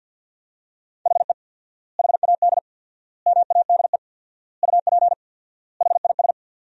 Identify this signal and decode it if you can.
Morse code, heard as a keyed tone, about 50 wpm, 700 Hz. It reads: HE HAD MADE UP HIS
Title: 4x Word-Spacing - Sentences from Top 500 Words - 50wpm